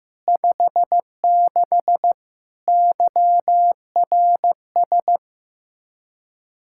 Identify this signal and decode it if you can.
Morse code, heard as a keyed tone, about 15 wpm, 700 Hz. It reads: 56 YRS
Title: Practice QSO - Exchange 3 - More Info - 15wpm